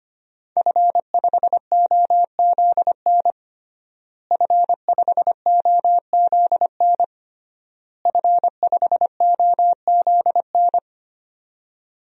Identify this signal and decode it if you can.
Morse code, heard as a keyed tone, about 25 wpm, 700 Hz. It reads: F5OZN F5OZN F5OZN
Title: DXCC-100 Call Signs - 5 Characters - Repeated 3x - 25wpm